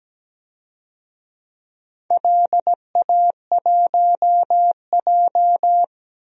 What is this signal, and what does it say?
Morse code, heard as a keyed tone, about 17 wpm, 700 Hz. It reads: LA1J
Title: DXCC-100 Call Signs - 4 Characters - Repeated 3x - 17wpm